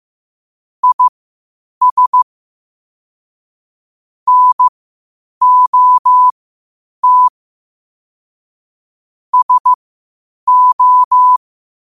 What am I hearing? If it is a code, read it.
Morse code, heard as a keyed tone, about 15 wpm, spaced as for 8 wpm, 1000 Hz. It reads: IS NOT SO
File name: qotdfiles / 8wpm